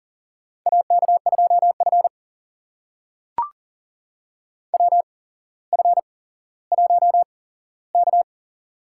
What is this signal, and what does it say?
Morse code, heard as a keyed tone, about 40 wpm, spaced as for 17 wpm, 700 Hz. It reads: AK2F WF1K
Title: USA Call Signs - 4 Characters - 8x Character Spacing - 40wpm